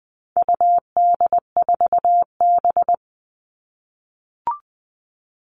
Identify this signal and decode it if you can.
Morse code, heard as a keyed tone, about 20 wpm, 700 Hz. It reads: UD4B
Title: All Cut Numbers - 4-Digit 20wpm